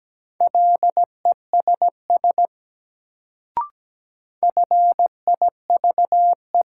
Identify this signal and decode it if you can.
Morse code, heard as a keyed tone, about 17 wpm, 700 Hz. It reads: LESS FIVE
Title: Sentences from Top 400 Words - New Word Drill - 17wpm